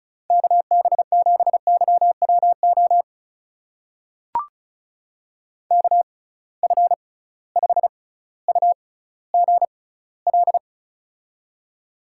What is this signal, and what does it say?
Morse code, heard as a keyed tone, about 36 wpm, spaced as for 18 wpm, 700 Hz. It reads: KB7YWO KF5UGL
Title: USA Call Signs - 6 Characters - 6x Character Spacing - 35wpm